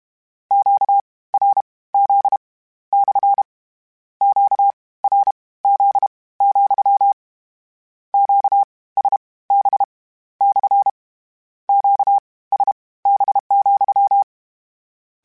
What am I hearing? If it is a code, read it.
Morse code, heard as a keyed tone, about 32 wpm, spaced as for 19 wpm, 800 Hz. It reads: QRZ / QRZ, QSB / QSB,
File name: Q Calls / Q Calls-mp3s / Q CALLS SPSL @30